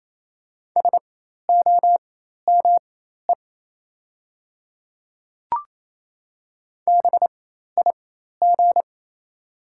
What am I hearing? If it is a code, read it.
Morse code, heard as a keyed tone, about 28 wpm, spaced as for 13 wpm, 690 Hz. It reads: SOME BIG
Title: 4x Character Spacing - Top 300 Words - 28wpm